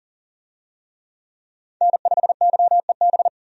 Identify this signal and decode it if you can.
Morse code, heard as a keyed tone, about 40 wpm, 700 Hz. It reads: N5YEB
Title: USA Call Signs - 5 Characters - 40wpm